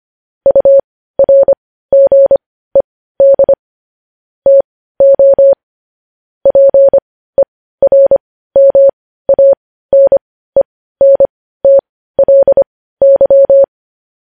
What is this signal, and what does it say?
Morse code, heard as a keyed tone, about 25 wpm, spaced as for 15 wpm, 550 Hz. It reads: URGED TO PERMANENTLY